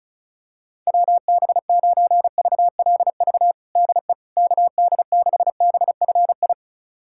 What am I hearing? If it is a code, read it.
Morse code, heard as a keyed tone, about 36 wpm, 700 Hz. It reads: WB9VLV DE KD6BFI
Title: USA Callsign DE Callsign 35wpm